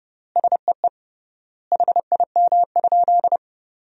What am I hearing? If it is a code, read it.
Morse code, heard as a keyed tone, about 32 wpm, 700 Hz. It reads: SEE HIM?
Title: 3x Word-Spacing - Sentences from Top 100 Words - 30wpm